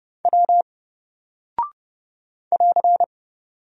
Morse code, heard as a keyed tone, about 28 wpm, 700 Hz. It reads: W <AR>
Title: Single Character (Letter:Number:Punctuation:Prosign) 30wpm